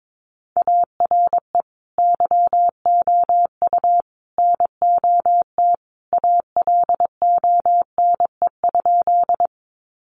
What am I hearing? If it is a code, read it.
Morse code, heard as a keyed tone, about 22 wpm, 700 Hz. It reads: ARE YOU NOT ALONE?
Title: Sentences from Top 1000 Words 22wpm